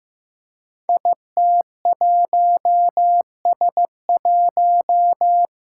Morse code, heard as a keyed tone, about 15 wpm, 700 Hz. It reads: IT1S1
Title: Five Challenging Letter-Number Combinations 15wpm